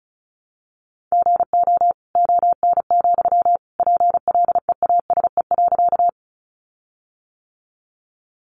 Morse code, heard as a keyed tone, about 34 wpm, 700 Hz. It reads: GO ON, PLEASE.
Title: Sentences from Top 1500 Words 35wpm